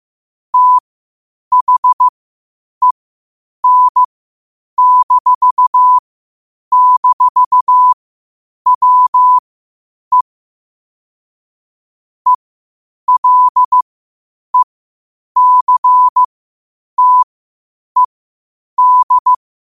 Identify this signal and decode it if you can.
Morse code, heard as a keyed tone, about 15 wpm, spaced as for 8 wpm, 1000 Hz. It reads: THEN--WE ELECTED
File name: qotdfiles / 8wpm